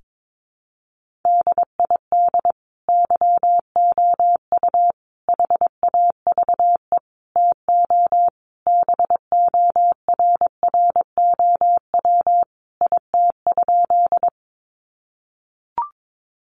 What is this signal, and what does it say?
Morse code, heard as a keyed tone, about 22 wpm, 700 Hz. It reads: DID YOU HAVE TO BORROW IT?
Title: Sentences from Top 1500 Words 22wpm